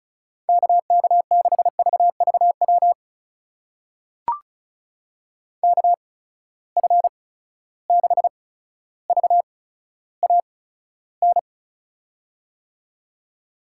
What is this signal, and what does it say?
Morse code, heard as a keyed tone, about 34 wpm, spaced as for 15 wpm, 700 Hz. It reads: KK6VVW KF6VAN
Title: USA Call Signs - 6 Characters - 8x Character Spacing - 35wpm